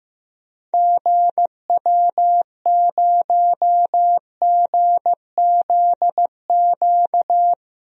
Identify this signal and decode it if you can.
Morse code, heard as a keyed tone, about 15 wpm, 700 Hz. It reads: GW0GZQ
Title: DXCC-100 Call Signs - 6 Characters - 15wpm